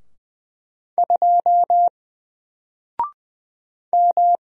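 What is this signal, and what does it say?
Morse code, heard as a keyed tone, about 20 wpm, 700 Hz. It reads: 2 M
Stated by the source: Single letter and number Morse drill at 20 wpm, from N4LCD.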